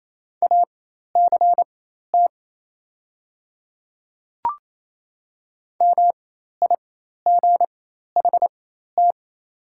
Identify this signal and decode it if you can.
Morse code, heard as a keyed tone, about 28 wpm, spaced as for 13 wpm, 700 Hz. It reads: ACT MIGHT
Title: 4x Character Spacing - Top 300 Words - 28wpm